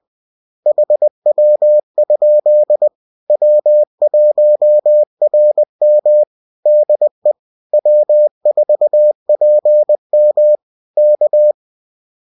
Morse code, heard as a keyed tone, about 20 wpm, 600 Hz. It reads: HW? W1RM DE W4PM K